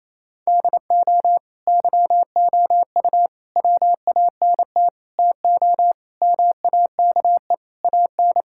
Morse code, heard as a keyed tone, about 28 wpm, 700 Hz. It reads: DO YOU WANT TO MAKE AN
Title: Sentences from Top 1000 Words - Review Part 6 - 28wpm